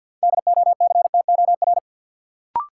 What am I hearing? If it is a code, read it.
Morse code, heard as a keyed tone, about 53 wpm, 700 Hz. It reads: DOCTOR
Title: Sentences from Top 1500 Words - New Word Drill - 50wpm